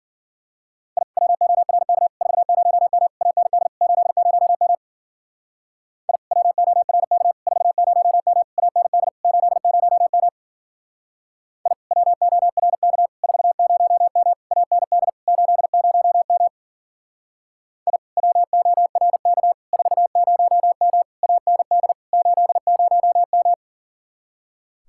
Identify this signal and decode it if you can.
Morse code, heard as a keyed tone, about 49 wpm, 700 Hz. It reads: I WORK 40M AND 80M I WORK 40M AND 80M I WORK 40M AND 80M I WORK 40M AND 80M
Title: Speed-Racing Common QSO Phrases - 40wpm